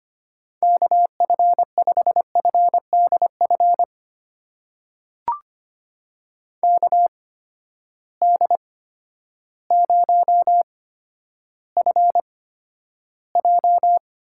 Morse code, heard as a keyed tone, about 25 wpm, spaced as for 11 wpm, 700 Hz. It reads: KF5FDF KD0FJ
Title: USA Call Signs - 6 Characters - 8x Character Spacing - 25wpm